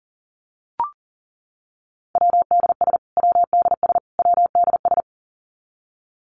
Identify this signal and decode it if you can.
Morse code, heard as a keyed tone, about 40 wpm, 700 Hz. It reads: WDS WDS WDS
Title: Abbreviations Repeated 3x - 40wpm